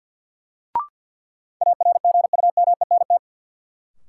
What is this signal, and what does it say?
Morse code, heard as a keyed tone, about 53 wpm, 700 Hz. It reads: ARGUMENT
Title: Sentences from Top 1400 Words - New Word Drill - 50wpm